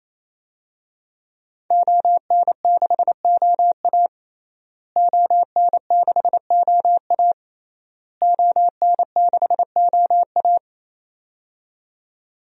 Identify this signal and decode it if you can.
Morse code, heard as a keyed tone, about 27 wpm, 700 Hz. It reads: ON6OA ON6OA ON6OA
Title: DXCC-100 Call Signs - 5 Characters - Repeated 3x - 28wpm